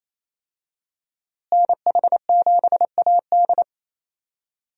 Morse code, heard as a keyed tone, about 28 wpm, 700 Hz. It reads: NH7AD
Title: USA Call Signs - 5 Characters - Repeated 3x - 28wpm